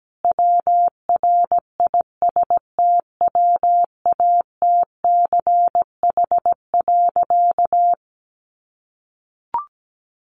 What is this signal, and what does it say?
Morse code, heard as a keyed tone, about 17 wpm, 700 Hz. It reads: WRISTWATCH.
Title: A Sound Of Thunder 17wpm